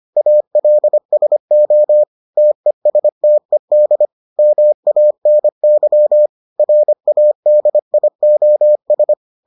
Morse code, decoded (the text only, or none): ALSO TESTED MANY RADIOS